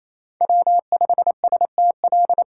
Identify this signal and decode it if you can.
Morse code, heard as a keyed tone, about 28 wpm, 700 Hz. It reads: W5STL
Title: USA Call Signs - 5 Characters - 8x Character Spacing - 28wpm